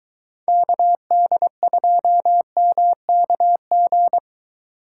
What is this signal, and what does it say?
Morse code, heard as a keyed tone, about 21 wpm, 700 Hz. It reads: KD2MKG